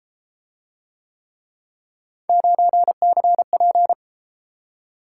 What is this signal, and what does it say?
Morse code, heard as a keyed tone, about 32 wpm, 700 Hz. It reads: 9CP